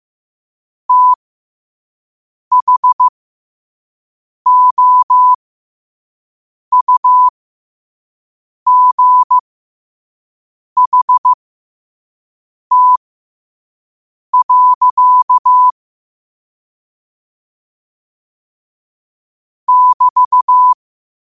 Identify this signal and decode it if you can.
Morse code, heard as a keyed tone, about 15 wpm, spaced as for 5 wpm, 1000 Hz. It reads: THOUGHT. <BT>
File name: qotdfiles / 5wpm